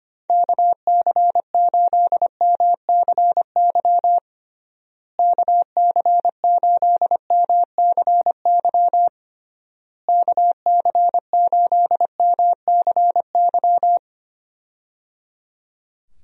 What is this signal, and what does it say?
Morse code, heard as a keyed tone, about 25 wpm, 700 Hz. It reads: KC8MCY KC8MCY KC8MCY
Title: USA Call Signs - 6 Characters - Repeated 3x - 25wpm